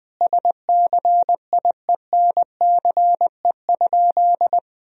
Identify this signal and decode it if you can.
Morse code, heard as a keyed tone, about 20 wpm, 700 Hz. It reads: SCIENCE?
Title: Sentences from Top 1000 Words - Review Part 2 - 20wpm